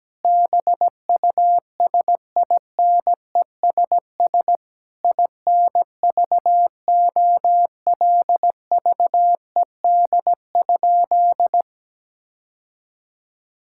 Morse code, heard as a keyed tone, about 17 wpm, 700 Hz. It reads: BUSINESS INVOLVED?
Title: Sentences from Top 1000 Words - Review Part 5 - 17wpm